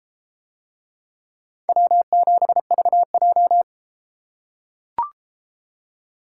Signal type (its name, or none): Morse code